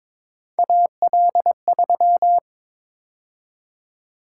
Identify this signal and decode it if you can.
Morse code, heard as a keyed tone, about 26 wpm, 700 Hz. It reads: AL3
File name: Speed-Racing - Three Letter-Number Combinations - 22wpm